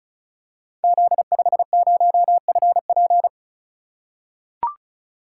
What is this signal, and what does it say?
Morse code, heard as a keyed tone, about 36 wpm, 700 Hz. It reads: Z50FP
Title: Code Groups - 5 Letter:Numbers - 35wpm